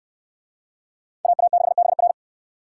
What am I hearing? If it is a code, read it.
Morse code, heard as a keyed tone, about 63 wpm, 700 Hz. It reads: AA6BR